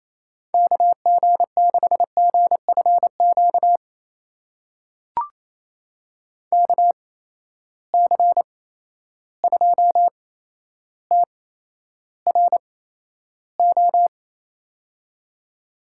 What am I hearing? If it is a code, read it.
Morse code, heard as a keyed tone, about 28 wpm, spaced as for 12 wpm, 700 Hz. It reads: KG6GFQ KC2TRO
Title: USA Call Signs - 6 Characters - 8x Character Spacing - 28wpm